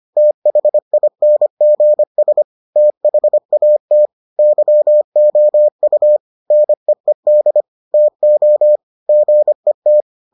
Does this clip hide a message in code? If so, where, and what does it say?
Yes: Morse code from 0.2 to 10.0 s, heard as a keyed tone, about 25 wpm, 600 Hz. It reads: THINGS THAT YOU NEED TO GET